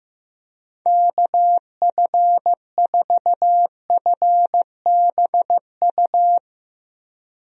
Morse code, heard as a keyed tone, about 15 wpm, 700 Hz. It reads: KF4FBU
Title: USA Call Signs - 6 Characters - Repeated 2x - 15wpm